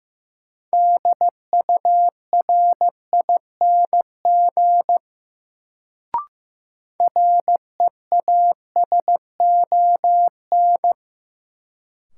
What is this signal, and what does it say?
Morse code, heard as a keyed tone, about 15 wpm, 700 Hz. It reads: DURING REASON
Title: Sentences from Top 700 Words - New Word Drill - 15wpm